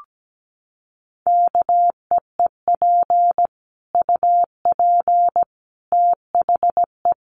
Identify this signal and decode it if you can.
Morse code, heard as a keyed tone, about 17 wpm, 700 Hz. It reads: KEEP UP THE
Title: Sentences from Top 1300 Words 17wpm